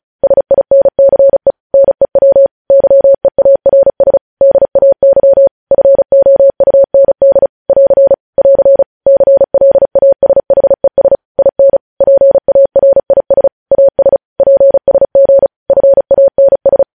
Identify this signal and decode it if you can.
Morse code, heard as a keyed tone, about 36 wpm, 550 Hz. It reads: SINCE NEW YEARS DAY FOUND <AR> <AR> CLASHES IN PARIS AS PSG FANS
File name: podcast / News headlines-1x-35x35